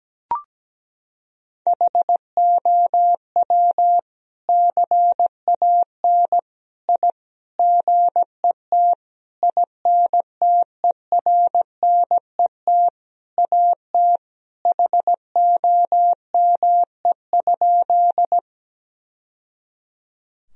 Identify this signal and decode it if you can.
Morse code, heard as a keyed tone, about 17 wpm, 700 Hz. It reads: HOW CAN I GET INTERNET AT HOME?
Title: Sentences from Top 1000 Words - Review Part 4 - 17wpm